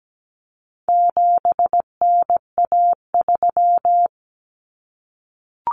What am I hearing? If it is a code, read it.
Morse code, heard as a keyed tone, about 17 wpm, 700 Hz. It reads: 7NA3
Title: Common Cut Numbers - 4-Digit 17wpm